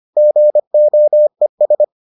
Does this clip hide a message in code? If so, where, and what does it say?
Yes: Morse code from 0.2 to 1.8 s, heard as a keyed tone, about 25 wpm, 600 Hz. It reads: GOES